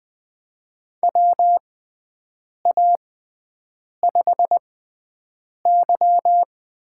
Morse code, heard as a keyed tone, about 20 wpm, spaced as for 7 wpm, 700 Hz. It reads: WA5Y
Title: USA Call Signs - 6 Characters - 6x Character Spacing - 20wpm